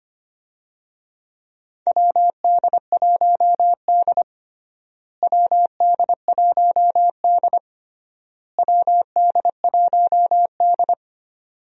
Morse code, heard as a keyed tone, about 25 wpm, 700 Hz. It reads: WD1D WD1D WD1D
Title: USA Call Signs - 4 Characters - Repeated 3x - 25wpm